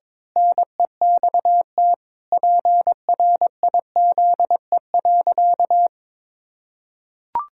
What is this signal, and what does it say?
Morse code, heard as a keyed tone, about 22 wpm, 700 Hz. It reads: NEXT PRIZE.
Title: Pangram Challenge v2 - 22wpm